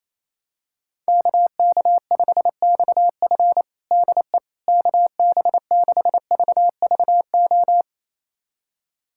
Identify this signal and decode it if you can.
Morse code, heard as a keyed tone, about 28 wpm, 700 Hz. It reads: KK5XF DE KB6VVO
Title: USA Callsign DE Callsign 28wpm